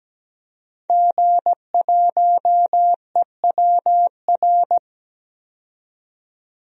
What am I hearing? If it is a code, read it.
Morse code, heard as a keyed tone, about 17 wpm, 700 Hz. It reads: G1EWR
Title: DXCC-100 Call Signs - 5 Characters - 17wpm